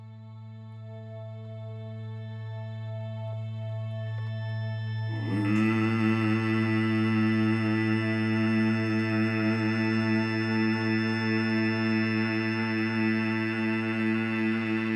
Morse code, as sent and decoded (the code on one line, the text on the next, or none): none